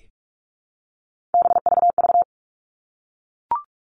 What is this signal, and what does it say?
Morse code, heard as a keyed tone, about 44 wpm, 700 Hz. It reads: BVV
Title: All Cut Numbers - 3-Digit 45wpm